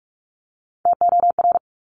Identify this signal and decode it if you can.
Morse code, heard as a keyed tone, about 46 wpm, 700 Hz. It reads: TOR
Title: Top Three Letter Combinations in Words 45wpm